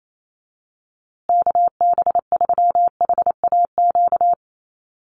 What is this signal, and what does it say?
Morse code, heard as a keyed tone, about 28 wpm, 700 Hz. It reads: KB3HAQ
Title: USA Call Signs Repeated 2x 28wpm